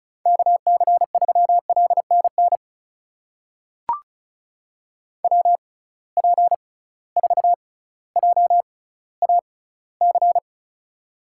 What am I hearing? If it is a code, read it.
Morse code, heard as a keyed tone, about 36 wpm, spaced as for 18 wpm, 700 Hz. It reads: KC3LNN WP4JAC
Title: USA Call Signs - 6 Characters - 6x Character Spacing - 35wpm